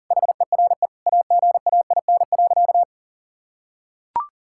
Morse code, heard as a keyed tone, about 40 wpm, 700 Hz. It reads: HERE AGAIN.
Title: Sentences from Top 1600 Words 40wpm